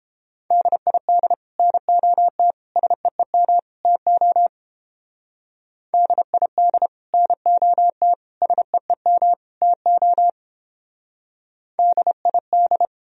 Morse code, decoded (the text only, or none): DID NOT SEEM TO DID NOT SEEM TO DID